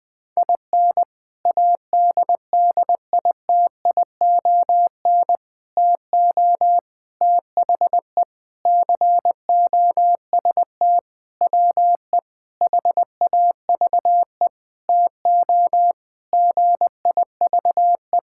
Morse code, heard as a keyed tone, about 21 wpm, 700 Hz. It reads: IN ADDITION TO THE COST WE HAVE TO GIVE
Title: Sentences with 4-Word Phrases - 20wpm